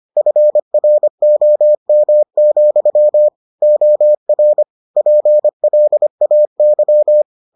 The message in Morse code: FROM, OR PLAY

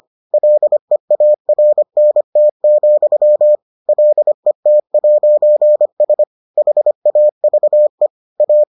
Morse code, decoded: LEARNT, LET'S HAVE A

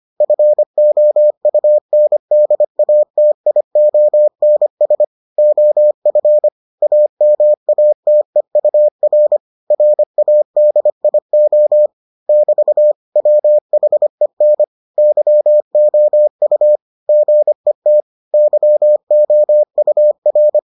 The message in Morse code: FOUNDATIONS OF AMATEUR RADIO <BT> WHEN YOU GET YOUR

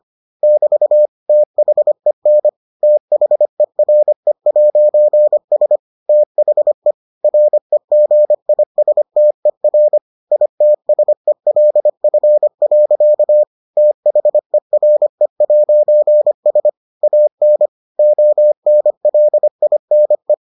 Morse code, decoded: <BT> THEN THERE'S THE REGISTER ITSELF. THERE'S AN ONLINE

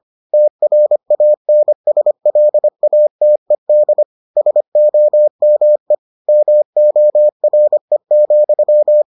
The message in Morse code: TRANSLATED SOME MORE,

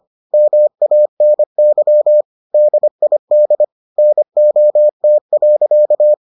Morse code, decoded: MANY DID NOT.

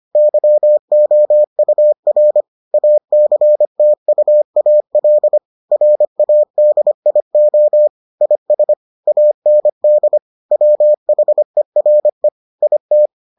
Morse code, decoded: YOUR ACTUAL RADIO IS AND WHERE IT

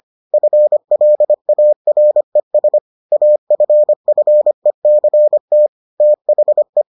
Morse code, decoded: FLARES AFFECT THE

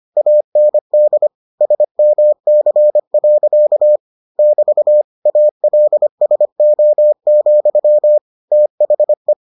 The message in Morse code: AND SMC. <BT> ALSO, THE

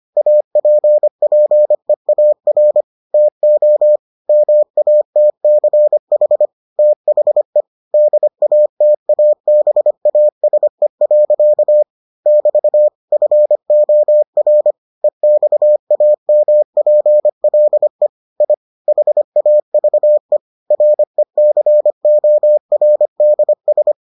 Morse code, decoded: APPEAR TO MATCH THE DATABASE. <BT> FOR EXAMPLE I HAVE RECORDS

